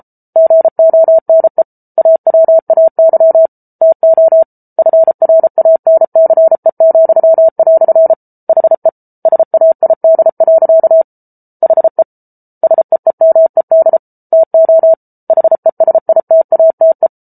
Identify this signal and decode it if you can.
Morse code, heard as a keyed tone, about 34 wpm, 650 Hz. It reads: GONE AWAY TO FRANCE," HE SAID. HE SEEMED TO HESITATE